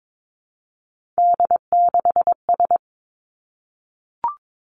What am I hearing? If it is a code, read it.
Morse code, heard as a keyed tone, about 22 wpm, 700 Hz. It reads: D6S
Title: Three Challenging Letter-Number Combinations 22wpm